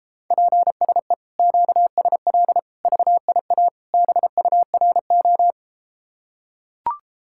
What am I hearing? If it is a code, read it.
Morse code, heard as a keyed tone, about 33 wpm, 700 Hz. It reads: PSE QSL VIA BURO